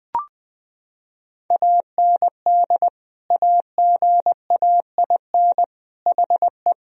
Morse code, heard as a keyed tone, about 20 wpm, 700 Hz. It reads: AND AGAIN HE
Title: Sentences from Top 1000 Words - Review Part 2 - 20wpm